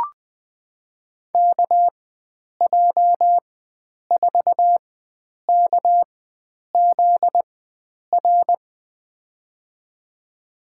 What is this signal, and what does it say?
Morse code, heard as a keyed tone, about 20 wpm, spaced as for 9 wpm, 700 Hz. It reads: KJ4KZR
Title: USA Call Signs - 6 Characters - 4x Character Spacing - 20wpm